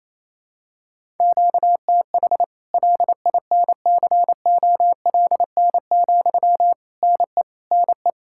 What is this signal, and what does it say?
Morse code, heard as a keyed tone, about 28 wpm, 700 Hz. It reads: QTH LINCOLN, NE NE
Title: Practice QSO - Exchange 1 - Beginning - 28wpm